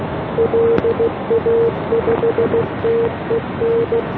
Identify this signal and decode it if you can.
Morse code, heard as a keyed tone, about 16 wpm, 470 Hz, with noise. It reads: LA5TEN